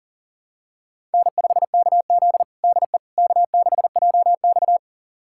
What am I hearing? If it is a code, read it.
Morse code, heard as a keyed tone, about 37 wpm, 700 Hz. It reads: N5KZ DE K6JX
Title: USA Callsign DE Callsign - 4 Characters - 40wpm